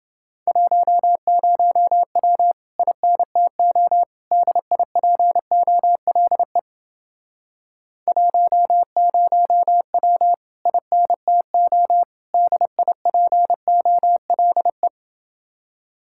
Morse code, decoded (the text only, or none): 10W INTO DIPOLE 10W INTO DIPOLE